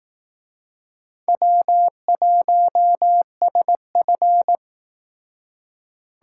Morse code, heard as a keyed tone, about 19 wpm, 700 Hz. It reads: W1SF